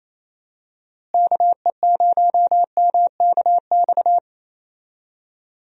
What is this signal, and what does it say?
Morse code, heard as a keyed tone, about 28 wpm, 700 Hz. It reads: KE0MKX
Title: USA Call Signs - 6 Characters - Repeated 3x - 28wpm